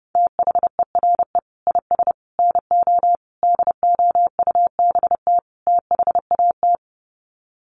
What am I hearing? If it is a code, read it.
Morse code, heard as a keyed tone, about 30 wpm, 700 Hz. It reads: THERE IS NO DOUBT THAT
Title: Sets of 5 Words 30wpm